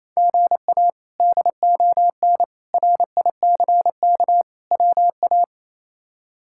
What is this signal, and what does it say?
Morse code, heard as a keyed tone, about 28 wpm, 700 Hz. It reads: GA DON RICK WA
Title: K1USN SST QSO Practice - 28wpm